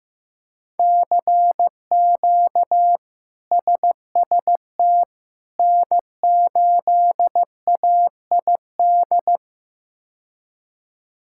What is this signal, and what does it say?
Morse code, heard as a keyed tone, about 16 wpm, 700 Hz. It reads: CQ SST N8AID
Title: K1USN SST QSO Practice - 15wpm